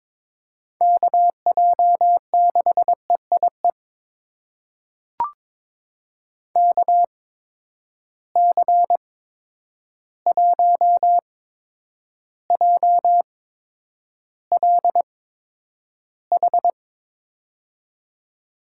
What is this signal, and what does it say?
Morse code, heard as a keyed tone, about 22 wpm, spaced as for 9 wpm, 700 Hz. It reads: KJ6EIE KC1JLH
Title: USA Call Signs - 6 Characters - 8x Character Spacing - 22wpm